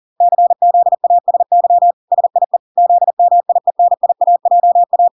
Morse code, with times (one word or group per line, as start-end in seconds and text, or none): 0.2-1.9 CZASY
2.1-2.6 SIE
2.8-5.1 ZMIENIAJA